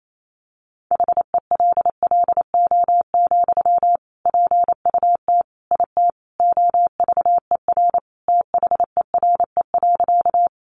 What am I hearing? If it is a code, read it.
Morse code, heard as a keyed tone, about 28 wpm, 700 Hz. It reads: HELLO, PUT IT OVER THERE.
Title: Sentences from Top 1100 Words 28wpm